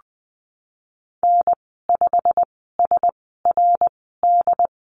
Morse code, heard as a keyed tone, about 19 wpm, spaced as for 14 wpm, 700 Hz. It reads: N5SRD
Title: 173 Lesson 28g - Number 9 - Call Signs 20wpm